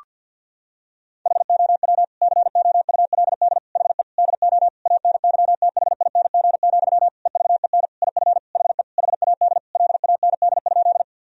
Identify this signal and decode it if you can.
Morse code, heard as a keyed tone, about 50 wpm, 700 Hz. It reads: HOW COULD HE DO ANYTHING, EVEN IF HE HAD LAND?
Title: Sentences from Top 1000 Words - Review Part 1 - 50wpm